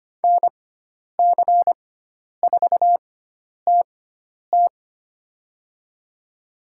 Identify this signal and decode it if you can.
Morse code, heard as a keyed tone, about 25 wpm, spaced as for 10 wpm, 700 Hz. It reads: NC4TT
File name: Lesson 16g - Letter c - Call Signs 25wpm